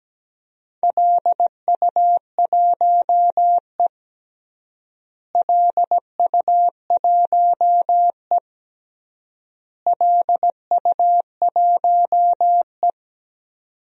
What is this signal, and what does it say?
Morse code, heard as a keyed tone, about 17 wpm, 700 Hz. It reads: LU1E LU1E LU1E
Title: DXCC-100 Call Signs - 4 Characters - Repeated 3x - 17wpm